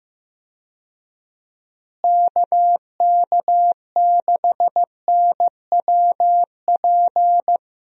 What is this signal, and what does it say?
Morse code, heard as a keyed tone, about 15 wpm, 700 Hz. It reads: KK6NWP
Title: USA Call Signs - 6 Characters - Repeated 2x - 15wpm